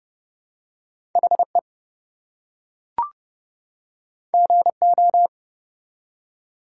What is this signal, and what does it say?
Morse code, heard as a keyed tone, about 29 wpm, 700 Hz. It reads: HE GO
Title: Two Letter Words - 30wpm